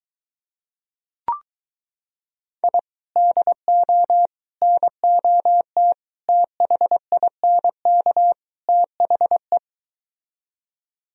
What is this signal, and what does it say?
Morse code, heard as a keyed tone, about 23 wpm, 700 Hz. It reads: I DO NOT THINK THE